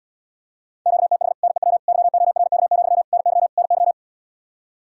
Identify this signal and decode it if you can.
Morse code, heard as a keyed tone, about 79 wpm, 700 Hz. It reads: QTH NEW YORK, NY NY